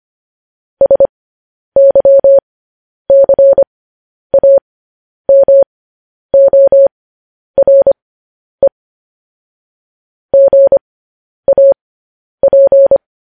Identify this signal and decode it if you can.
Morse code, heard as a keyed tone, about 25 wpm, spaced as for 10 wpm, 550 Hz. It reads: SYCAMORE GAP
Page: news-headlines-in-morse-code-at-10-wpm